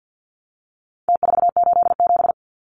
Morse code, heard as a keyed tone, about 50 wpm, 700 Hz. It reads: T487
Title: Numbers - Contest Style - 4-Digit 50wpm